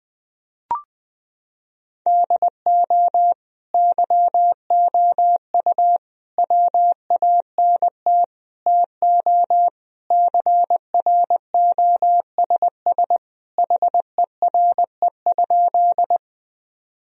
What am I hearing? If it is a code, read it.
Morse code, heard as a keyed tone, about 20 wpm, 700 Hz. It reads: DO YOU WANT TO CROSS HERE?
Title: Sentences from Top 1000 Words - Review Part 1 - 20wpm